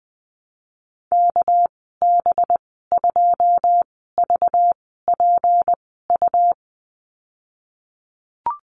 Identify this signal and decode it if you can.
Morse code, heard as a keyed tone, about 20 wpm, spaced as for 15 wpm, 700 Hz. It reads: KB2VPU
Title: 200 Lesson 32g - Letter k - Call Signs 20wpm